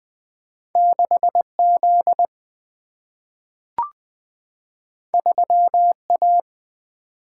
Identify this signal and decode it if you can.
Morse code, heard as a keyed tone, about 20 wpm, 700 Hz. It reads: 6 Z 3 A
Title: Single-Letter Single-Number 20wpm